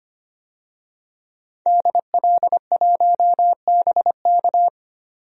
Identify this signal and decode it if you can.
Morse code, heard as a keyed tone, about 25 wpm, 700 Hz. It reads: DL1BK
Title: DXCC-100 Call Signs - 5 Characters - 25wpm